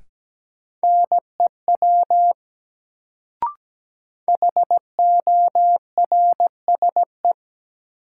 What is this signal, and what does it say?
Morse code, heard as a keyed tone, about 17 wpm, 700 Hz. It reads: NEW HORSE
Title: Sentences from Top 300 Words - New Word Drill - 17wpm